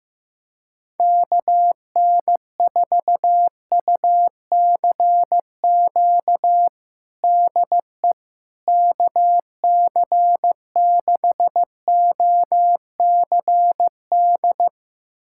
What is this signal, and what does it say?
Morse code, heard as a keyed tone, about 15 wpm, 700 Hz. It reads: KN4UCQ DE KC6OCD
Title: USA Callsign DE Callsign - 6 Characters - 15wpm